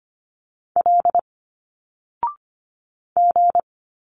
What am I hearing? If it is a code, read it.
Morse code, heard as a keyed tone, about 25 wpm, 700 Hz. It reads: L G
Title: Single Letters 25wpm